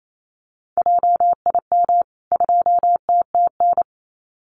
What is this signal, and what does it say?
Morse code, heard as a keyed tone, about 28 wpm, 700 Hz. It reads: JIM 2TTN
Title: CWT Exchange 28wpm